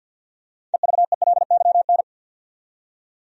Morse code, heard as a keyed tone, about 52 wpm, 700 Hz. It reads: EVELYN
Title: USA Common Names - 50wpm